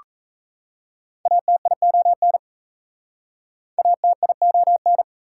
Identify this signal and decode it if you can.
Morse code, heard as a keyed tone, about 38 wpm, 690 Hz. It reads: ATION ATION